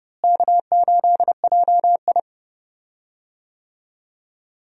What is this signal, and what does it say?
Morse code, heard as a keyed tone, about 31 wpm, 700 Hz. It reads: K8JI
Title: USA Call Signs - 4 Characters - Repeated 3x - 30wpm